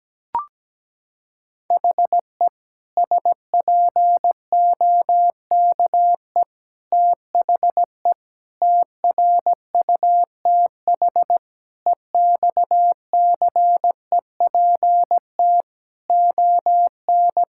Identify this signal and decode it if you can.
Morse code, heard as a keyed tone, about 17 wpm, 700 Hz. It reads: HE SPOKE THE TRUTH EXCEPT ON
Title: Sentences from Top 1600 Words 17wpm